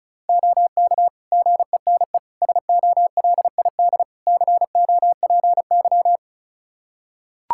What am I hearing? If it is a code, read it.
Morse code, heard as a keyed tone, about 36 wpm, 700 Hz. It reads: OK GENE SOLID COPY
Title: Practice QSO - Exchange 3 - More Info - 35wpm